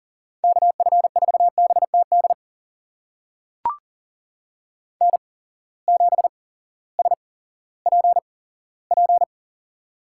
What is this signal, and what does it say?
Morse code, heard as a keyed tone, about 42 wpm, spaced as for 19 wpm, 690 Hz. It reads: KF4BTD N7SPP
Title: USA Call Signs - 8x Character Spacing - 40wpm